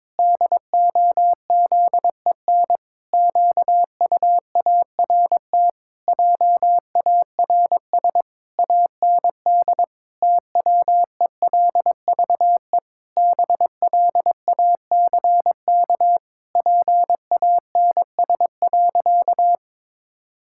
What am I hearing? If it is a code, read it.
Morse code, heard as a keyed tone, about 22 wpm, 700 Hz. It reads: DOZEN QUART JARS AND TWELVE BLACK PANS.
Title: Pangram Challenge v2 - 22wpm